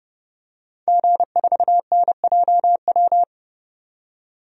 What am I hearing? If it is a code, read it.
Morse code, heard as a keyed tone, about 30 wpm, 700 Hz. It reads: G4NJW
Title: DXCC-100 Call Signs - 5 Characters - 30wpm